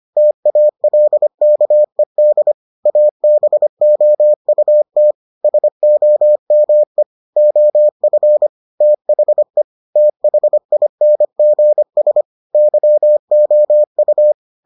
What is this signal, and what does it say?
Morse code, heard as a keyed tone, about 25 wpm, 600 Hz. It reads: TALKED ABOUT SOME OF THE THINGS YOU